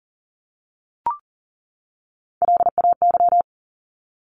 Morse code, heard as a keyed tone, about 40 wpm, 700 Hz. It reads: LAY
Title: Top 500 Words 40wpm